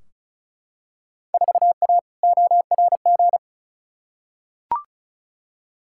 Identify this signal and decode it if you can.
Morse code, heard as a keyed tone, about 34 wpm, 700 Hz. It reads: 4A ORG
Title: Field Day Exchanges - 35wpm